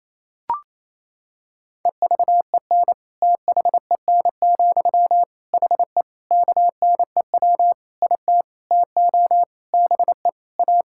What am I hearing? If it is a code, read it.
Morse code, heard as a keyed tone, about 28 wpm, 700 Hz. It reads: EVEN THEN, HE KNEW IT TO BE A
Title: Sentences from Top 1600 Words 28wpm